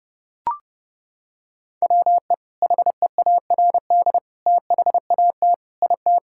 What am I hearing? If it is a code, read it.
Morse code, heard as a keyed tone, about 30 wpm, 700 Hz. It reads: WE HEARD THAT IT
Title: Sentences from Top 1000 Words - Review Part 6 - 30wpm